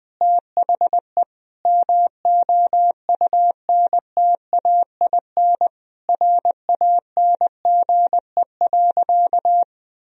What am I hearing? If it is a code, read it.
Morse code, heard as a keyed tone, about 19 wpm, 700 Hz. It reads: THE MOUNTAIN RANGE.
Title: Sentences from Top 1000 Words - Review Part 4 - 20wpm